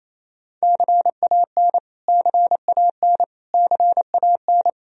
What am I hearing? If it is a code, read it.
Morse code, heard as a keyed tone, about 28 wpm, 700 Hz. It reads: CAN CAN CAN
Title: Top 100 Words Repeated 28wpm